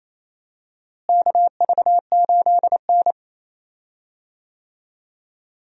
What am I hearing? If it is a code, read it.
Morse code, heard as a keyed tone, about 28 wpm, 700 Hz. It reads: KV8N
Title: USA Call Signs - 4 Characters - Repeated 3x - 28wpm